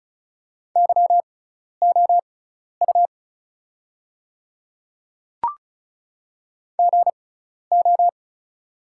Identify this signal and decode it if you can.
Morse code, heard as a keyed tone, about 34 wpm, spaced as for 12 wpm, 700 Hz. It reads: YOU GO
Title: 6x Character Spacing - Top 500 Words - 35wpm